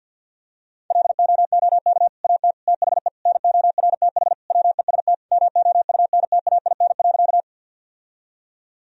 Morse code, heard as a keyed tone, about 52 wpm, 700 Hz. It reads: LOOK AT THE NORTH WEST MOUNTAIN.
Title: Sentences from Top 1000 Words - Review Part 2 - 50wpm